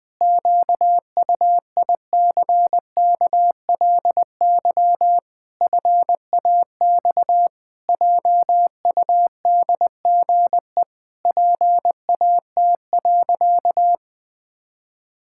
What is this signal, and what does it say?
Morse code, heard as a keyed tone, about 20 wpm, 700 Hz. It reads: QUICKLY FAX JUDGE PAT.
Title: Pangram Challenge v2 - 20wpm